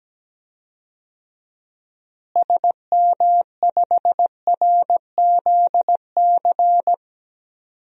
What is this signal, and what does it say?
Morse code, heard as a keyed tone, about 17 wpm, 700 Hz. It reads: SM5RZC
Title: DXCC-100 Call Signs - 6 Characters - Repeated 2x - 17wpm